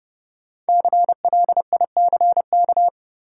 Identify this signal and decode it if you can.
Morse code, heard as a keyed tone, about 31 wpm, 700 Hz. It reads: CLICK
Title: Sentences from Top 1100 Words - New Word Drill - 30wpm